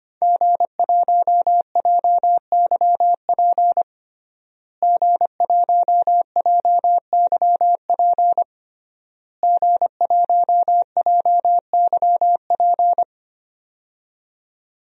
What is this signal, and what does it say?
Morse code, heard as a keyed tone, about 25 wpm, 700 Hz. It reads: G1JYP G1JYP G1JYP
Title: DXCC-100 Call Signs - 5 Characters - Repeated 3x - 25wpm